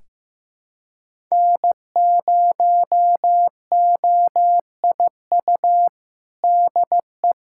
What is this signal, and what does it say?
Morse code, heard as a keyed tone, about 15 wpm, 700 Hz. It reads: N0OIU DE
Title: USA Callsign DE Callsign 15wpm